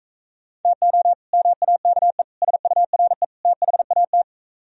Morse code, heard as a keyed tone, about 43 wpm, 700 Hz. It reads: TO MAKE SURE THAT